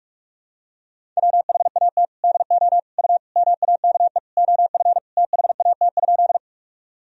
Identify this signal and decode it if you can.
Morse code, heard as a keyed tone, about 42 wpm, 700 Hz. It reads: WHAT DO U MAKE OF THAT?
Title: Sentences from Top 100 Words and Abbreviations 45wpm